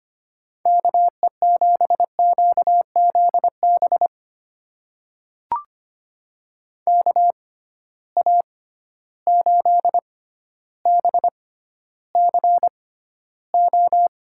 Morse code, heard as a keyed tone, about 25 wpm, spaced as for 13 wpm, 700 Hz. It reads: KE7QZB KA8BCO
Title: USA Call Signs - 6 Characters - 6x Character Spacing - 25wpm